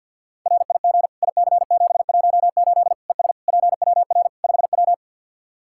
Morse code, heard as a keyed tone, about 51 wpm, 700 Hz. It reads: RIG IC718 ES PWR 5W
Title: Practice QSO - Exchange 2 - Info - 50wpm